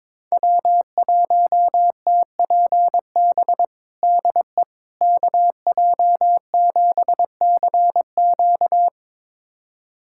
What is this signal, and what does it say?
Morse code, heard as a keyed tone, about 22 wpm, 700 Hz. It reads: W1TPB DE KJ7CQ
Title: USA Callsign DE Callsign - 5 Characters - 22wpm